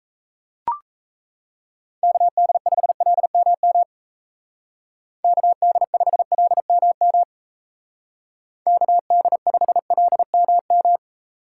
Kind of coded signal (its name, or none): Morse code